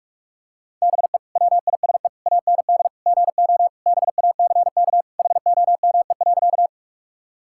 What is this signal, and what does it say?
Morse code, heard as a keyed tone, about 43 wpm, 700 Hz. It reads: BE WISE AND GO BACK HOME.
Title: Sentences from Top 1600 Words 45wpm